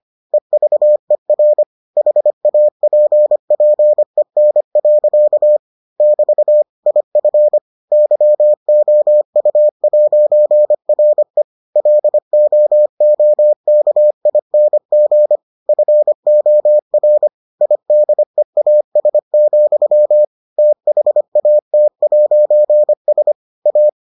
Morse code, heard as a keyed tone, about 25 wpm, 600 Hz. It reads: EVER HAPPEN. <BT> IF YOU'RE LOOKING FOR IDEAS, THAT'S A